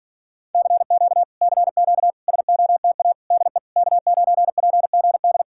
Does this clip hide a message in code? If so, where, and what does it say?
Yes: Morse code from 0.5 to 5.4 s, heard as a keyed tone, about 48 wpm, 700 Hz. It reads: CQ CQ SOTA DE K9PGD